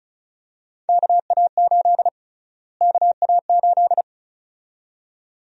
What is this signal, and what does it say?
Morse code, heard as a keyed tone, about 36 wpm, 700 Hz. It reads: KA8 KA8
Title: USA Call Sign Prefixes - Repeated 2x - 35wpm